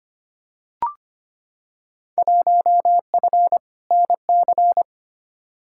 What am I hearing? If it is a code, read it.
Morse code, heard as a keyed tone, about 25 wpm, 700 Hz. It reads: 1F NC
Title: Field Day Exchanges - 25wpm